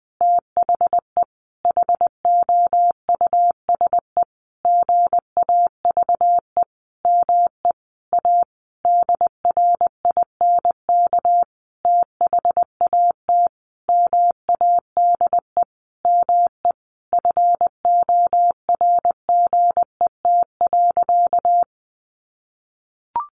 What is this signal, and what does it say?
Morse code, heard as a keyed tone, about 20 wpm, 700 Hz. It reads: THE HOUSE GAVE ME A DRINK THAT MADE ME FORGET.
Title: Sentences from Top 1500 Words 20wpm